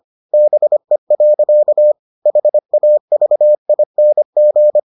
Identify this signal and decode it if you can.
Morse code, heard as a keyed tone, about 25 wpm, 600 Hz. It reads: BE. HAVING